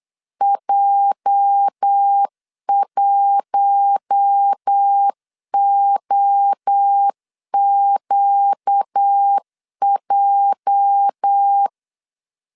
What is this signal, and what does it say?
Morse code, heard as a keyed tone, about 8 wpm, 800 Hz. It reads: J1OQJ
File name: LEZIONI / 8 YJ45 / 8 YJ 8wpm 3313